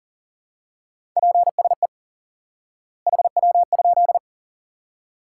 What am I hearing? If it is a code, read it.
Morse code, heard as a keyed tone, about 40 wpm, 700 Hz. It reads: PSE HW?
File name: IQE Course 3 - Lesson 89 - YAGI - Two Elements - 40wpm